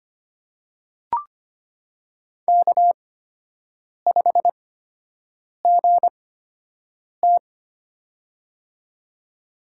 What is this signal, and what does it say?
Morse code, heard as a keyed tone, about 25 wpm, spaced as for 7 wpm, 700 Hz. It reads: K5GT
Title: USA Call Signs - 8x Character Spacing - 25wpm